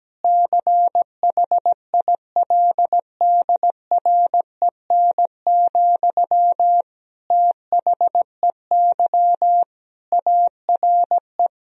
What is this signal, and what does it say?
Morse code, heard as a keyed tone, about 17 wpm, 700 Hz. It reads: CHILDREN, THEY ARE
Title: Sentences from Top 1000 Words - Review Part 3 - 17wpm